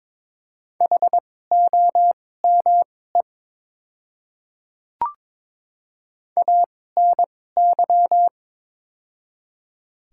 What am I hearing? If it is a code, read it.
Morse code, heard as a keyed tone, about 22 wpm, spaced as for 16 wpm, 700 Hz. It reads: HOME ANY
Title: 2x Character Spacing - Top 200 Words - 22wpm